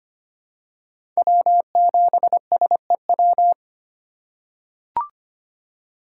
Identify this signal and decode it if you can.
Morse code, heard as a keyed tone, about 25 wpm, 700 Hz. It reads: W7SEW
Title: USA Call Signs - 5 Characters - 4x Character Spacing - 25wpm